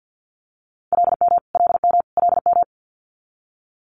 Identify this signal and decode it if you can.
Morse code, heard as a keyed tone, about 50 wpm, 690 Hz. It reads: LM LM LM
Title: Multi-letter Phonemes Repeated 3x 50wpm